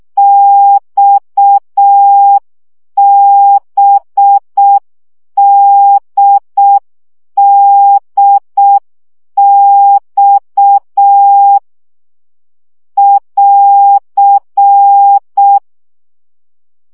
Morse code, heard as a keyed tone, about 6 wpm, 800 Hz. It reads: XBDDX <AR>
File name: CW gyakorlas d b x 5 baud